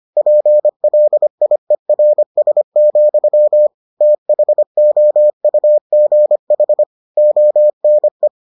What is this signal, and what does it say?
Morse code, heard as a keyed tone, about 25 wpm, 600 Hz. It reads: PLIERS, THOUGH ONE